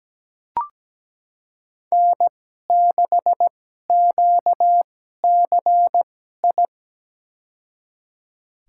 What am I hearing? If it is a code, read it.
Morse code, heard as a keyed tone, about 17 wpm, spaced as for 12 wpm, 700 Hz. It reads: N6QCI
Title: USA Call Signs - 5 Characters - 2x Character Spacing - 17wpm